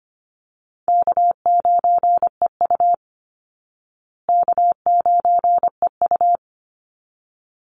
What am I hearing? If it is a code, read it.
Morse code, heard as a keyed tone, about 25 wpm, 700 Hz. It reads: K9EU K9EU
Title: Easy Call Signs Repeated 2x - 25wpm